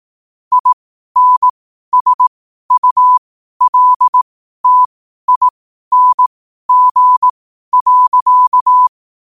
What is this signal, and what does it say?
Morse code, heard as a keyed tone, about 18 wpm, spaced as for 12 wpm, 1000 Hz. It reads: INSULTING.
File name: qotdfiles / 12wpm